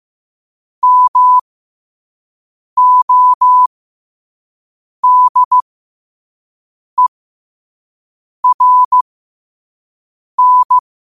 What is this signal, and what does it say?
Morse code, heard as a keyed tone, about 15 wpm, spaced as for 5 wpm, 1000 Hz. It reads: MODERN
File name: qotdfiles / 5wpm